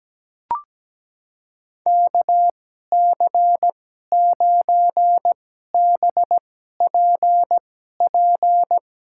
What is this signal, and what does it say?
Morse code, heard as a keyed tone, about 17 wpm, spaced as for 12 wpm, 700 Hz. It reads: KC9BPP
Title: USA Call Signs - 6 Characters - 2x Character Spacing - 17wpm